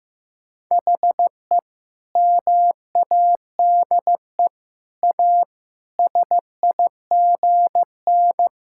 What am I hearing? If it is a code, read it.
Morse code, heard as a keyed tone, about 16 wpm, 700 Hz. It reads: HE MADE A SIGN
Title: Sentences from Top 1000 Words - Review Part 6 - 15wpm